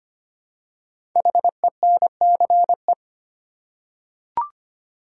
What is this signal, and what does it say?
Morse code, heard as a keyed tone, about 25 wpm, 700 Hz. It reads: HENCE
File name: 5 Letter Words - First 3 Letters Given - 25wpm